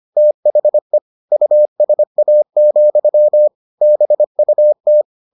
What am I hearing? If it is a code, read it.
Morse code, heard as a keyed tone, about 25 wpm, 600 Hz. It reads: THE USA, BUT